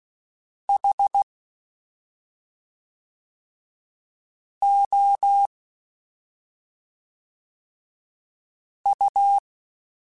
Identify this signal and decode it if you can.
Morse code, heard as a keyed tone, about 16 wpm, 800 Hz. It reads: H O U
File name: Special INGs / Special INGs-mp3s / SlowToFaster ing @5